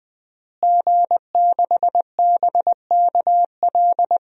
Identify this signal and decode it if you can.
Morse code, heard as a keyed tone, about 20 wpm, 700 Hz. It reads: G6BKL
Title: DXCC-100 Call Signs - 5 Characters - Repeated 3x - 20wpm